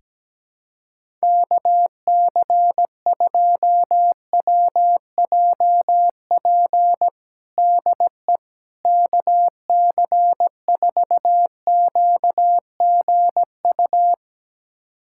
Morse code, heard as a keyed tone, about 17 wpm, 700 Hz. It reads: KC2WJP DE KC4QGU
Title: USA Callsign DE Callsign - 6 Characters - 17wpm